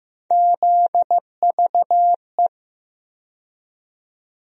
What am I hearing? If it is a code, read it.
Morse code, heard as a keyed tone, about 15 wpm, 700 Hz. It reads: ZVE
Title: USA Call Sign Postfixes - 3 Letters 15wpm